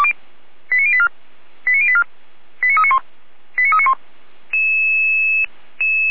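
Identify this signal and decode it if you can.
Five-tone selective call (ZVEI), heard as quick runs of five tones, tones of about 80 ms. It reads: 89084 89084 89391 89391
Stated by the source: ZVEI five-tone selective call for address 89036, received by SDR at feuerwehr-orlamuende (Orlamünde fire brigade).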